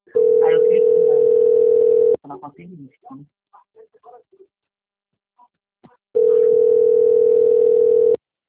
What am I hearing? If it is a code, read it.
Ringback tone (North America), 440 + 480 Hz, with speech.
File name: out- - - - - .37897